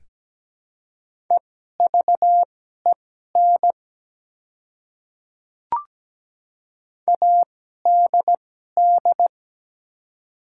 Morse code, heard as a keyed tone, about 17 wpm, spaced as for 12 wpm, 700 Hz. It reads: EVEN ADD
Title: 2x Character Spacing - Top 300 Words - 17wpm